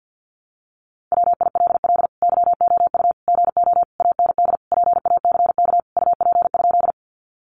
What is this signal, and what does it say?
Morse code, heard as a keyed tone, about 50 wpm, 700 Hz. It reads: WILL YOU GO AND PACK UP?
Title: Sentences from Top 1300 Words 50wpm